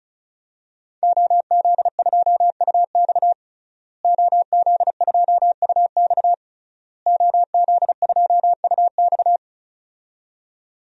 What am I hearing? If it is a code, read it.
Morse code, heard as a keyed tone, about 34 wpm, 700 Hz. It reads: OZ2UX OZ2UX OZ2UX
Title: DXCC-100 Call Signs - 5 Characters - Repeated 3x - 35wpm